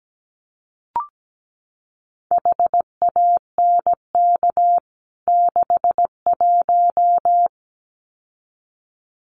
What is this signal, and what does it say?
Morse code, heard as a keyed tone, about 17 wpm, 700 Hz. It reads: HANK 61
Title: CWT Exchange 17wpm